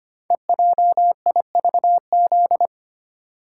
Morse code, heard as a keyed tone, about 25 wpm, 700 Hz. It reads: EJIVZ
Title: Code Groups - 5 Letters - 25wpm